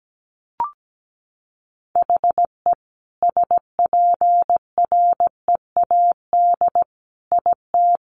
Morse code, heard as a keyed tone, about 17 wpm, 700 Hz. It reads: HE SPREAD IT
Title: Sentences from Top 1200 Words 17wpm